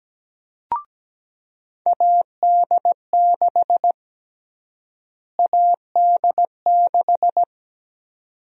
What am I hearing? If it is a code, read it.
Morse code, heard as a keyed tone, about 17 wpm, 700 Hz. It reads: AD6 AD6
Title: USA Call Sign Prefixes - Repeated 2x - 17wpm